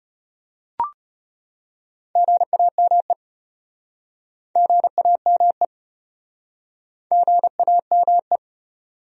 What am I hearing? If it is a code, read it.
Morse code, heard as a keyed tone, about 35 wpm, 700 Hz. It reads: GAME GAME GAME